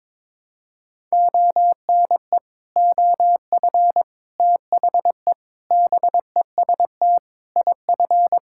Morse code, heard as a keyed tone, about 21 wpm, 700 Hz. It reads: ONE OF THE BEST IF
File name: practice / 22wpm / Sets of 8 Words - 22wpm